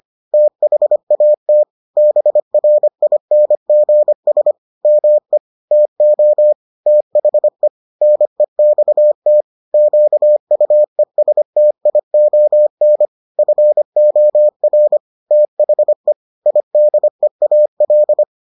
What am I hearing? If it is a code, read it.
Morse code, heard as a keyed tone, about 25 wpm, 600 Hz. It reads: THAT BRINGS ME TO THE NEXT QUESTION FOR THE IDEAL